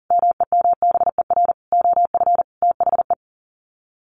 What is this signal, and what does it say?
Morse code, heard as a keyed tone, about 40 wpm, 700 Hz. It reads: MEMBER OF THE
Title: Sets of 5 Words 40wpm